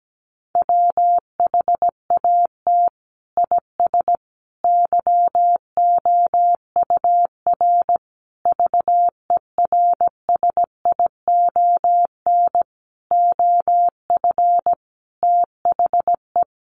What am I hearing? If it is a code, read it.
Morse code, heard as a keyed tone, about 17 wpm, 700 Hz. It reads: WHAT IS YOUR VERSION OF THE
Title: Sentences from Top 1400 Words 17wpm